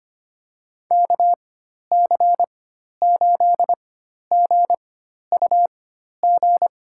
Morse code, heard as a keyed tone, about 25 wpm, spaced as for 12 wpm, 690 Hz. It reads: KC8GUG
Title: USA Call Signs - 4x Character Spacing - 25wpm